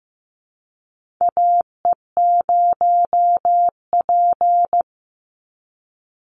Morse code, heard as a keyed tone, about 15 wpm, 700 Hz. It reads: AE0P
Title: Call Signs Repeated 3x - 15wpm